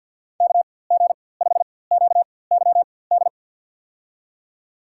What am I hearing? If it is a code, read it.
Morse code, heard as a keyed tone, about 47 wpm, spaced as for 23 wpm, 700 Hz. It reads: KG5QYD
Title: USA Call Signs - 4x Character Spacing - 50wpm